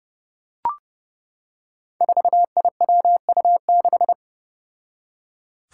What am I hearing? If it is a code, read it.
Morse code, heard as a keyed tone, about 30 wpm, 700 Hz. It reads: 4IWU6
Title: Five Challenging Letter-Number Combinations 30wpm